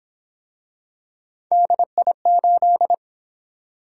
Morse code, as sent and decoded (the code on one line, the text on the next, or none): -.. .. ---..
DI8